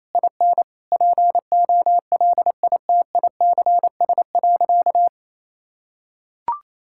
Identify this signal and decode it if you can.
Morse code, heard as a keyed tone, about 28 wpm, 700 Hz. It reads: IN POLITICS.
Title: Sentences from Top 1000 Words - Review Part 6 - 28wpm